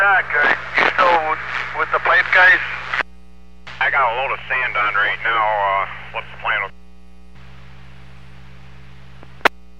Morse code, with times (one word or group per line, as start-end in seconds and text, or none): none